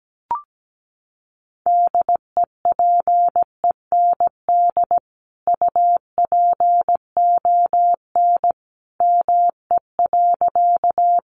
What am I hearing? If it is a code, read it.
Morse code, heard as a keyed tone, about 17 wpm, 700 Hz. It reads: DEPEND UPON ME.
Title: Sentences from Top 1300 Words 17wpm